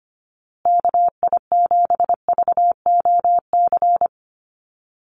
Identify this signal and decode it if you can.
Morse code, heard as a keyed tone, about 25 wpm, 700 Hz. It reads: KI7VOC
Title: USA Call Signs Repeated 2x 25wpm